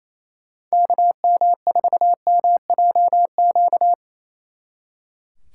Morse code, heard as a keyed tone, about 33 wpm, 700 Hz. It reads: KM4MJQ
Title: Speed Racing - USA 2x3 Call Signs 28wpm